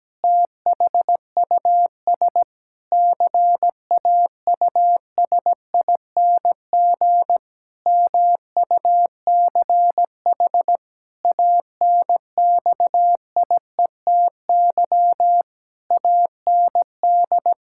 Morse code, heard as a keyed tone, about 17 wpm, 700 Hz. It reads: THUS CAUSING MUCH ANXIETY AND
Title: Pangram Challenge v2 - 17wpm